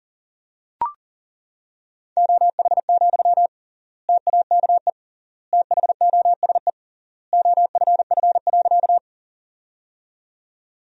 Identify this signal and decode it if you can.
Morse code, heard as a keyed tone, about 40 wpm, 700 Hz. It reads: OH, TAKE THOSE OFF.
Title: 3x Word-Spacing - Sentences from Top 500 Words - 40wpm